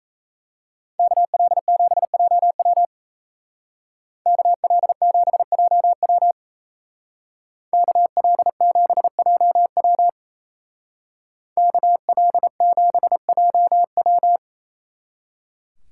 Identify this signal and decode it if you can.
Morse code, heard as a keyed tone, about 34 wpm, 700 Hz. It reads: KL7JW KL7JW KL7JW KL7JW
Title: Speed Racing - USA 2x2 Call Signs 28wpm